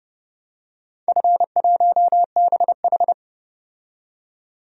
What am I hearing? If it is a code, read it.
Morse code, heard as a keyed tone, about 32 wpm, 700 Hz. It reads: F1BH
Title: DXCC-100 Call Signs - 4 Characters - 30wpm